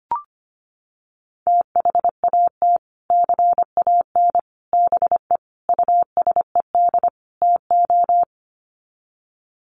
Morse code, heard as a keyed tone, about 25 wpm, 700 Hz. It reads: THAT CAN BE USED TO
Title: Sets of 5 Words Encore 25wpm